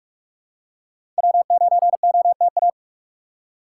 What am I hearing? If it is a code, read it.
Morse code, heard as a keyed tone, about 46 wpm, 700 Hz. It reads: W9OTA
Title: USA Call Signs - 5 Characters - 2x Character Spacing - 45wpm